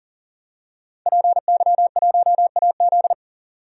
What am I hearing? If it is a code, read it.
Morse code, heard as a keyed tone, about 40 wpm, 700 Hz. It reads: PY1AZ
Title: DXCC-100 Call Signs - 5 Characters - 40wpm